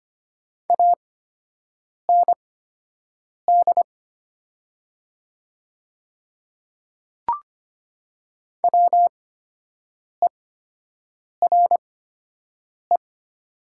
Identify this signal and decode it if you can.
Morse code, heard as a keyed tone, about 25 wpm, spaced as for 7 wpm, 700 Hz. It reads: AND WERE
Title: 8x Character Spacing - Top 100 Words - 25wpm